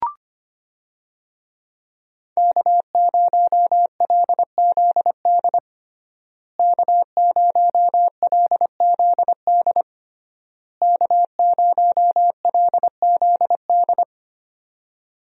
Morse code, heard as a keyed tone, about 25 wpm, 700 Hz. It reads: K0LZD K0LZD K0LZD
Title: USA Call Signs - 5 Characters - Repeated 3x - 25wpm